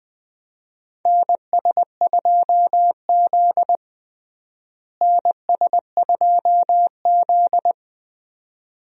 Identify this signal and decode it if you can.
Morse code, heard as a keyed tone, about 20 wpm, 700 Hz. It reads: NS2Z NS2Z
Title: USA Call Signs - 4 Characters - Repeated 2x - 20wpm